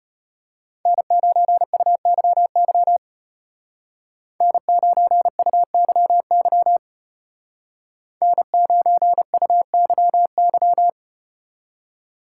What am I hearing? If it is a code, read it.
Morse code, heard as a keyed tone, about 34 wpm, 700 Hz. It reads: N9UYY N9UYY N9UYY